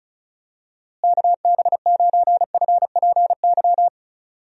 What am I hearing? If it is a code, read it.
Morse code, heard as a keyed tone, about 36 wpm, 700 Hz. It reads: KB9FPY
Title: USA Call Signs - 6 Characters - Repeated 3x - 35wpm